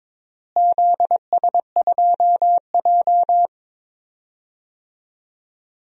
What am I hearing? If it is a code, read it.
Morse code, heard as a keyed tone, about 22 wpm, 700 Hz. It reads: ZS2J
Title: DXCC-100 Call Signs - 4 Characters - Repeated 3x - 22wpm